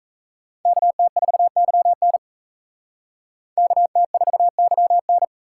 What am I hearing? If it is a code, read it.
Morse code, heard as a keyed tone, about 44 wpm, 700 Hz. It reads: KT4YN KT4YN